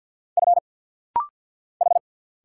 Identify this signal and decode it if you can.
Morse code, heard as a keyed tone, about 50 wpm, 700 Hz. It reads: F H
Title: Single Letter-Number Alt - Rapid-Fire 50wpm